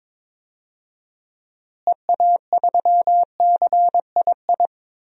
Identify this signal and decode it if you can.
Morse code, heard as a keyed tone, about 21 wpm, 700 Hz. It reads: EA3CII
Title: DXCC-100 Call Signs - 6 Characters - Repeated 2x - 22wpm